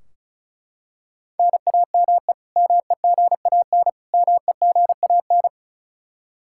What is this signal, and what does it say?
Morse code, heard as a keyed tone, about 36 wpm, 700 Hz. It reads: NAME MEGAN MEGAN
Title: Practice QSO - Exchange 1 - Beginning - 35wpm